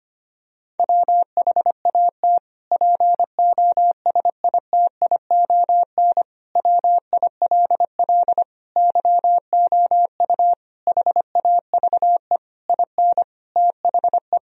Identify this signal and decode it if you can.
Morse code, heard as a keyed tone, about 25 wpm, 700 Hz. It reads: WHAT POSITION WILL YOU HAVE IN THE
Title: Sentences from Top 1000 Words - Review Part 3 - 25wpm